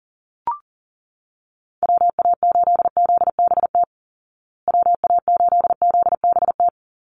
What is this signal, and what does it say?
Morse code, heard as a keyed tone, about 40 wpm, 700 Hz. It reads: WA8ZBT WA8ZBT
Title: Call Signs Repeated 3x - 40wpm